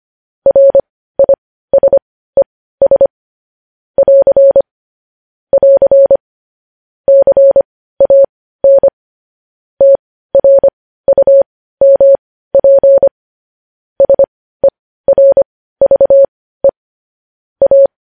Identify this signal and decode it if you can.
Morse code, heard as a keyed tone, about 25 wpm, spaced as for 15 wpm, 550 Hz. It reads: RISES <AR> <AR> CAN TRUMP SERVE A
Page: News Headlines in Morse Code at 15 WPM